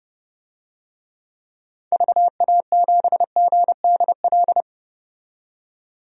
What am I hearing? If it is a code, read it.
Morse code, heard as a keyed tone, about 30 wpm, 700 Hz. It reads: VA7GDL
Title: DXCC-100 Call Signs - 6 Characters - 30wpm